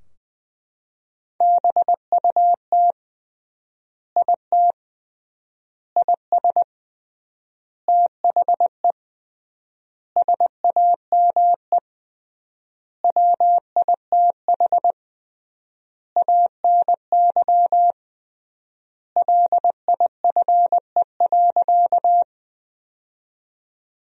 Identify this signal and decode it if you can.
Morse code, heard as a keyed tone, about 20 wpm, 700 Hz. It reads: BUT IT IS THE SAME WITH ANY LIFE.
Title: 3x Word-Spacing - Sentences from Top 300 Words - 20wpm